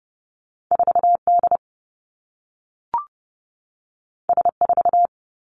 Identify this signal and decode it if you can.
Morse code, heard as a keyed tone, about 30 wpm, 700 Hz. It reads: 4 D S 4
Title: Two Challenging Letter-Number Combinations 30wpm